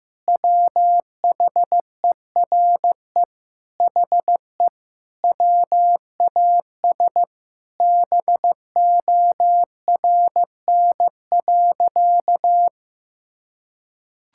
Morse code, heard as a keyed tone, about 15 wpm, 700 Hz. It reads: WHERE HE WAS BORN.
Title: Sentences from Top 1600 Words 15wpm